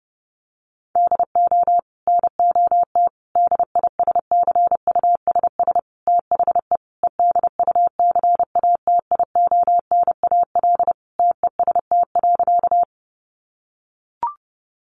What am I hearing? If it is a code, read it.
Morse code, heard as a keyed tone, about 30 wpm, 700 Hz. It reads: DO NOT DISCUSS THE EDUCATIONAL TEST.
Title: Sentences from Top 1200 Words 30wpm